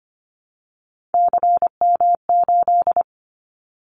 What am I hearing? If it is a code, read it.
Morse code, heard as a keyed tone, about 25 wpm, 700 Hz. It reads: CM8
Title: Callsign Prefixes for Top 100 DXCC - 25wpm